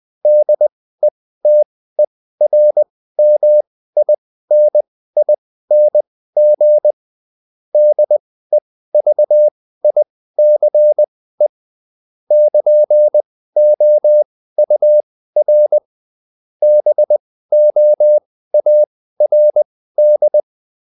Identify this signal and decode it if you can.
Morse code, heard as a keyed tone, about 20 wpm, spaced as for 14 wpm, 600 Hz. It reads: DETERMINING DEVICE <KN>OUR BOARD